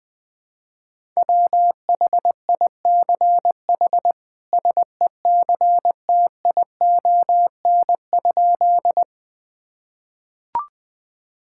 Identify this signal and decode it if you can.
Morse code, heard as a keyed tone, about 20 wpm, 700 Hz. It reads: WHICH SECTION?
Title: Sentences from Top 1000 Words - Review Part 5 - 20wpm